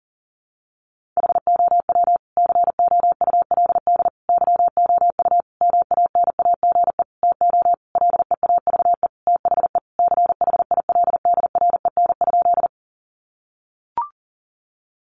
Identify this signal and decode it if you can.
Morse code, heard as a keyed tone, about 40 wpm, 700 Hz. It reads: HOW COULD YOU MANAGE TO LEAVE THE CHILDREN?
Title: Sentences from Top 1400 Words 40wpm